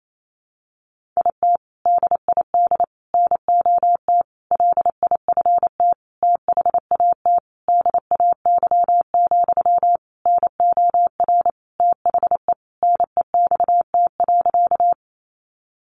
Morse code, heard as a keyed tone, about 28 wpm, 700 Hz. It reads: IT DID NOT LIFT THAT DAY, NOR THE NEXT.
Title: Sentences from Top 1500 Words 28wpm